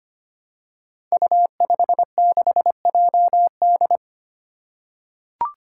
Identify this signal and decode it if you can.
Morse code, heard as a keyed tone, about 25 wpm, 700 Hz. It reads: U56JD
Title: Five Challenging Letter-Number Combinations 25wpm